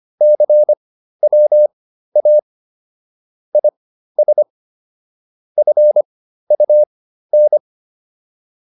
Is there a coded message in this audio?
Yes: Morse code from 0.2 to 7.6 s, heard as a keyed tone, about 25 wpm, spaced as for 13 wpm, 600 Hz. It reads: CWA IS FUN